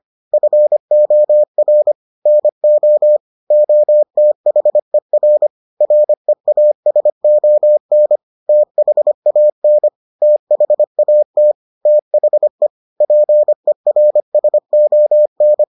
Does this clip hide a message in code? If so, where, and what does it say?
Yes: Morse code from 0.3 to 15.6 s, heard as a keyed tone, about 25 wpm, 600 Hz. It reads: FOR NO OTHER REASON THAN THAT THE PERSON